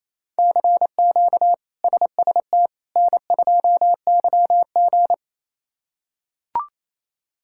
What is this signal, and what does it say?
Morse code, heard as a keyed tone, about 28 wpm, 700 Hz. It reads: CQ SST N2YG
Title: K1USN SST QSO Practice - 28wpm